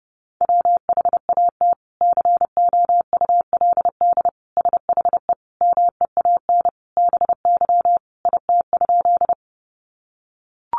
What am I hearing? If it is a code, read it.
Morse code, heard as a keyed tone, about 30 wpm, 700 Hz. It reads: WHAT COULD SHE MEAN BY IT?
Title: Sentences from Top 200 Words 30wpm